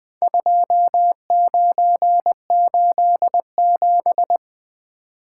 Morse code, heard as a keyed tone, about 20 wpm, 700 Hz. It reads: 2987
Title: Four Letter-Number Combinations - 20wpm